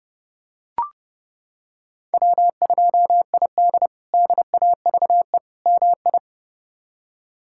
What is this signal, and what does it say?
Morse code, heard as a keyed tone, about 30 wpm, 700 Hz. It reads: W2ID DAVE MI
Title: K1USN SST QSO Practice - 30wpm